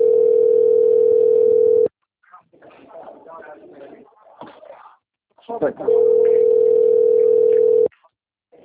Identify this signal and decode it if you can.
Ringback tone (North America), 440 + 480 Hz, with speech.
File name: out- - - - - .9460